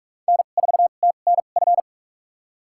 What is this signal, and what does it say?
Morse code, heard as a keyed tone, about 47 wpm, spaced as for 33 wpm, 690 Hz. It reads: N4TNF
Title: USA Call Signs - 5 Characters - 2x Character Spacing - 45wpm